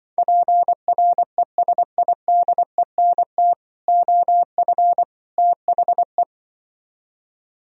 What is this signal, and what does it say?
Morse code, heard as a keyed tone, about 24 wpm, 700 Hz. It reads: PRESIDENT OF THE
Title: Speed-Racing - Sets of 3 Words 20wpm